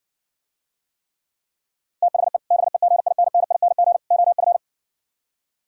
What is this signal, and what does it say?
Morse code, heard as a keyed tone, about 59 wpm, 700 Hz. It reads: THE BEGINNING OF